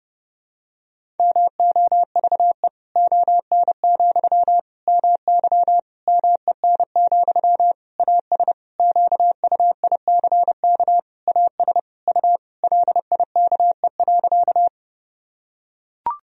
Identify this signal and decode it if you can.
Morse code, heard as a keyed tone, about 30 wpm, 700 Hz. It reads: MOVE ON, MY MEN, AS QUICK AS U LIKE.
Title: Sentences from Top 500 Words and Abbreviations 30wpm